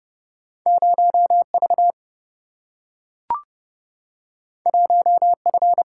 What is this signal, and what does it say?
Morse code, heard as a keyed tone, about 30 wpm, 700 Hz. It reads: 0V 1F
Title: Single-Letter Single-Number 30wpm